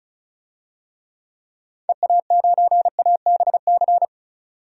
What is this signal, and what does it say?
Morse code, heard as a keyed tone, about 34 wpm, 700 Hz. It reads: EA9ABC
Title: DXCC-100 Call Signs - 6 Characters - 35wpm